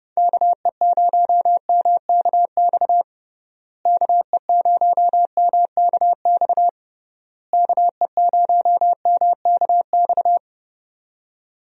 Morse code, heard as a keyed tone, about 30 wpm, 700 Hz. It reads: KE0MKX KE0MKX KE0MKX
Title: USA Call Signs - 6 Characters - Repeated 3x - 30wpm